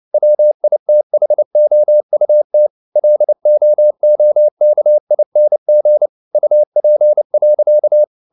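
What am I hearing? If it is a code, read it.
Morse code, heard as a keyed tone, about 29 wpm, 600 Hz. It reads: WITHOUT LOOKING UP.
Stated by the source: Morse code at 29 wpm, sending Ken Follett's novel Never in English, from CW med Gnister.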